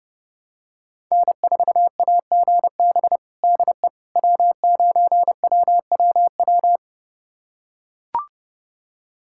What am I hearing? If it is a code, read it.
Morse code, heard as a keyed tone, about 30 wpm, 700 Hz. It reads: N4AGB DE W9WWW
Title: USA Callsign DE Callsign - 5 Characters - 30wpm